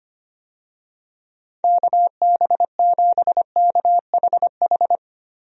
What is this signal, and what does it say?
Morse code, heard as a keyed tone, about 25 wpm, 700 Hz. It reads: KB7KHH
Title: USA Call Signs - 6 Characters - Repeated 2x - 25wpm